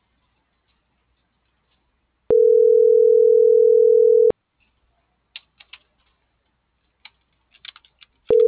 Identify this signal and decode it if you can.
Ringback tone (North America), 440 + 480 Hz.